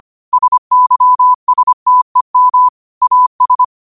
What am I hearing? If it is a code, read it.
Morse code, heard as a keyed tone, about 25 wpm, 1000 Hz. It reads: SYSTEM AS
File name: qotdfiles / 25wpm